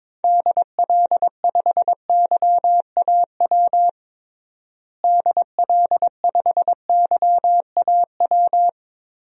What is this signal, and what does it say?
Morse code, heard as a keyed tone, about 21 wpm, 700 Hz. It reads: DL5YAW DL5YAW
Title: DXCC-100 Call Signs - 6 Characters - Repeated 3x - 22wpm